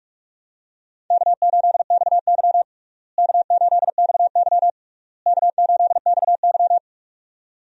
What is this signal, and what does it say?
Morse code, heard as a keyed tone, about 45 wpm, 700 Hz. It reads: K8XY K8XY K8XY
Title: USA Call Signs - 4 Characters - Repeated 3x - 45wpm